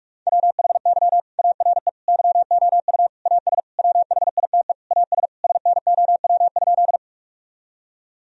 Morse code, heard as a keyed tone, about 41 wpm, 700 Hz. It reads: WHY ARE YOU AS WHITE AS SNOW?
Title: Sentences from Top 1000 Words - Review Part 3 - 45wpm